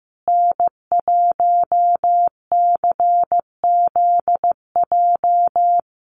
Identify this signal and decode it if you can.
Morse code, heard as a keyed tone, about 15 wpm, 700 Hz. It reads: N1CZJ
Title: USA Call Signs Repeated 3x 15wpm